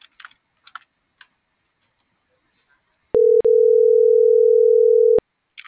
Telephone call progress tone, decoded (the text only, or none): ringback tone